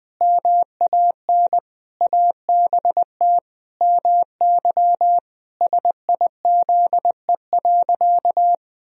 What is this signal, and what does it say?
Morse code, heard as a keyed tone, about 20 wpm, 700 Hz. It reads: MAN ABT MY SIZE.
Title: Sentences from Top 400 Words and Abbreviations 20wpm